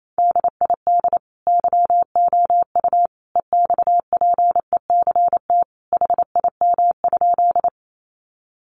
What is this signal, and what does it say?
Morse code, heard as a keyed tone, about 28 wpm, 700 Hz. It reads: DID YOU EXPECT HIM?
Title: Sentences from Top 1100 Words 28wpm